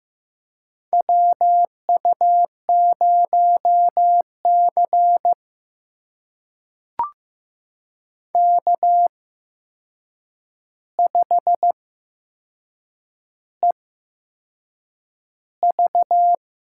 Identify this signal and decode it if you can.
Morse code, heard as a keyed tone, about 15 wpm, spaced as for 6 wpm, 700 Hz. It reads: WU0C K5EV
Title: USA Call Signs - 4 Characters - 8x Character Spacing - 15wpm